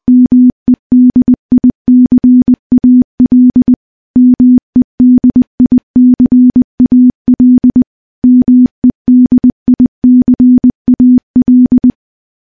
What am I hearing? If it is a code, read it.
Morse code, heard as a keyed tone, about 20 wpm, 250 Hz. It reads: MEDICAL MEDICAL MEDICAL
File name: morse-training-20WPM